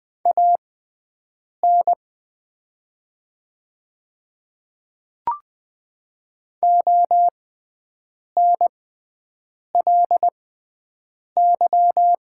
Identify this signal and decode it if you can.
Morse code, heard as a keyed tone, about 20 wpm, spaced as for 7 wpm, 700 Hz. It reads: AN ONLY
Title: 6x Character Spacing - Top 300 Words - 20wpm